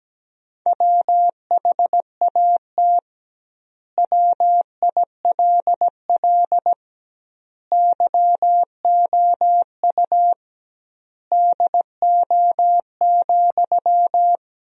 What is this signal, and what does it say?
Morse code, heard as a keyed tone, about 17 wpm, 700 Hz. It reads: WHAT WILL YOU DO,
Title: 2x Word-Spacing - Sentences from Top 100 Words - 17wpm